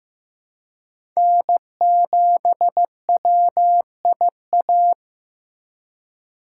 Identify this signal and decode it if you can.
Morse code, heard as a keyed tone, about 15 wpm, 700 Hz. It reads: N7WIA
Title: USA Call Signs - 5 Characters - 3x Character Spacing - 15wpm